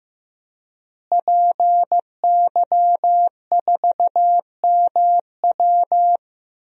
Morse code, heard as a keyed tone, about 15 wpm, 700 Hz. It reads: PY4MW
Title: DXCC-100 Call Signs - 5 Characters - Repeated 3x - 15wpm